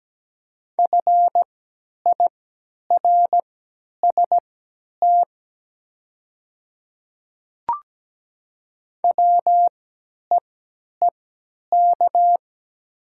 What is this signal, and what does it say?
Morse code, heard as a keyed tone, about 17 wpm, spaced as for 10 wpm, 700 Hz. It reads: FIRST WEEK
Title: 3x Character Spacing - Top 500 Words - 17wpm